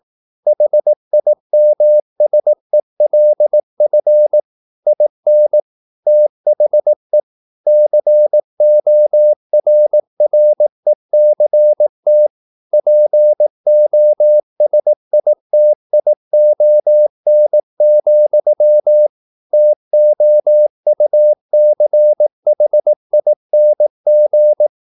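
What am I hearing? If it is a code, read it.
Morse code, heard as a keyed tone, about 18 wpm, 600 Hz. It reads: HIMSELF IN THE CORRECT POSITION, TOUCHING
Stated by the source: Morse practice at 18 wpm, English text, from CW med Gnister.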